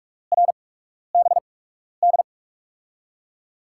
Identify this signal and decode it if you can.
Morse code, heard as a keyed tone, about 46 wpm, spaced as for 12 wpm, 700 Hz. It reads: RBD
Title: USA Call Signs - 5 Characters - 8x Character Spacing - 45wpm